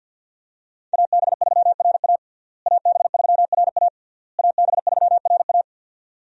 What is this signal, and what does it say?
Morse code, heard as a keyed tone, about 49 wpm, 700 Hz. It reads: AB3RA AB3RA AB3RA
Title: USA Call Signs - 5 Characters - Repeated 3x - 50wpm